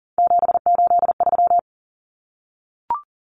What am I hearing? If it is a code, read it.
Morse code, heard as a keyed tone, about 40 wpm, 700 Hz. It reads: 783
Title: Numbers - 3-Digit 40wpm